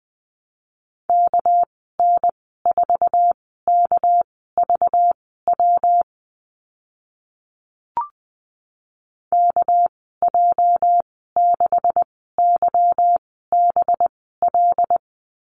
Morse code, heard as a keyed tone, about 20 wpm, spaced as for 14 wpm, 700 Hz. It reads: KN4KVW KJ6YBL
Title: 245 Lesson 39g - Letter z - Call Signs 20wpm